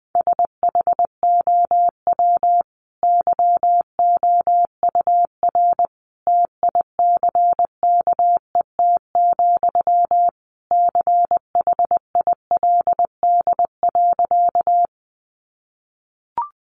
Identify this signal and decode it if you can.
Morse code, heard as a keyed tone, about 20 wpm, 700 Hz. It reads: SHOW YOUR TICKET, CHILD.
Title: Sentences from Top 1400 Words 20wpm